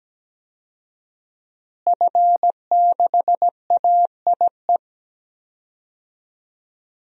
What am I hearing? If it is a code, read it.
Morse code, heard as a keyed tone, about 17 wpm, 700 Hz. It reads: F6AIE
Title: DXCC-100 Call Signs - 5 Characters - Repeated 3x - 17wpm